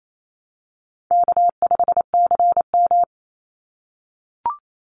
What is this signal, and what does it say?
Morse code, heard as a keyed tone, about 28 wpm, 700 Hz. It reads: K5CM
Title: Easy Call Signs - 28wpm